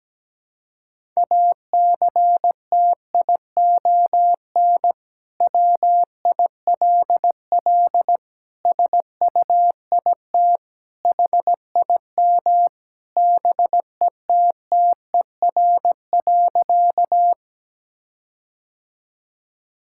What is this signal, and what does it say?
Morse code, heard as a keyed tone, about 17 wpm, 700 Hz. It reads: ACTION WILL SUIT HIM BETTER.
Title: Sentences from Top 1600 Words 17wpm